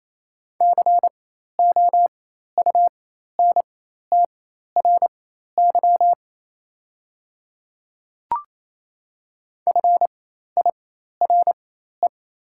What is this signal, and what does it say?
Morse code, heard as a keyed tone, about 28 wpm, spaced as for 13 wpm, 700 Hz. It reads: COUNTRY FIRE
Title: 4x Character Spacing - Top 500 Words - 28wpm